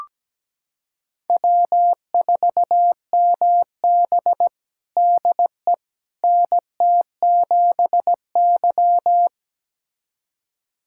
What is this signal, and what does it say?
Morse code, heard as a keyed tone, about 17 wpm, 700 Hz. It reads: W4MB DE NT7Y
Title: USA Callsign DE Callsign - 4 Characters - 17wpm